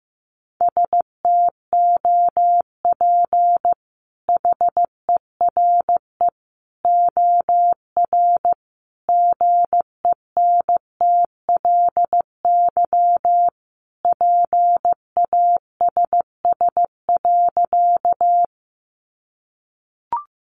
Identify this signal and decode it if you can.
Morse code, heard as a keyed tone, about 14 wpm, 700 Hz. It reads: STOP HERE OR GENTLY PASS.
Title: Sentences from Top 1500 Words 15wpm